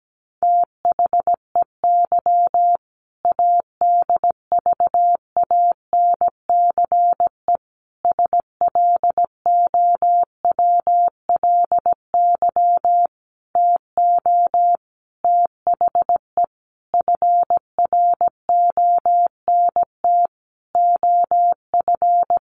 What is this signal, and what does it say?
Morse code, heard as a keyed tone, about 17 wpm, 700 Hz. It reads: THEY ADVANCE SLOWLY TO THE FRONT OF
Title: Sentences from Top 1400 Words 17wpm